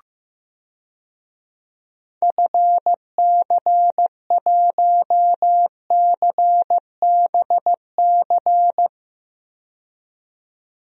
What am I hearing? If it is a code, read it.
Morse code, heard as a keyed tone, about 16 wpm, 690 Hz. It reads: FC1CBC
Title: DXCC-100 Call Signs - 6 Characters - 15wpm